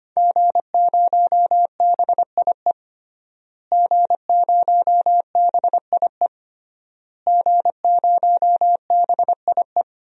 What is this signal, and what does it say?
Morse code, heard as a keyed tone, about 25 wpm, 700 Hz. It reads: G0BIE G0BIE G0BIE
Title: DXCC-100 Call Signs - 5 Characters - Repeated 3x - 25wpm